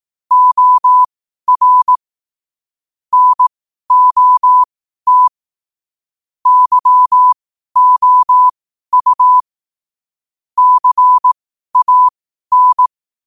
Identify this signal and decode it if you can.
Morse code, heard as a keyed tone, about 18 wpm, spaced as for 12 wpm, 1000 Hz. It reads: OR NOT YOU CAN
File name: qotdfiles / 12wpm